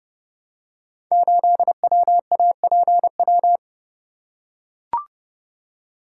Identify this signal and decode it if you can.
Morse code, heard as a keyed tone, about 30 wpm, 690 Hz. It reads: 8WAPW
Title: Code Groups - 5 Letter:Numbers - 30wpm